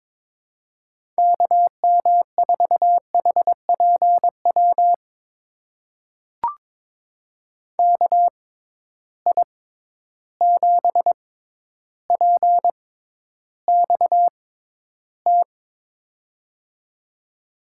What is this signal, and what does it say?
Morse code, heard as a keyed tone, about 22 wpm, spaced as for 11 wpm, 700 Hz. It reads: KM4HPW KI7PXT
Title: USA Call Signs - 6 Characters - 6x Character Spacing - 22wpm